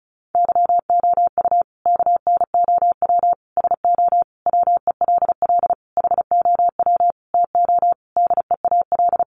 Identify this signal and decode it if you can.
Morse code, heard as a keyed tone, about 34 wpm, 700 Hz. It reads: YOU KNOW SO WELL HOW TO DEAL
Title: Sentences from Top 800 Words 35wpm